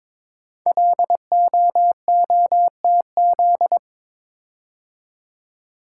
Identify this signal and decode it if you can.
Morse code, heard as a keyed tone, about 22 wpm, 700 Hz. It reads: LOOTZ
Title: Code Groups - 5 Letters - 22wpm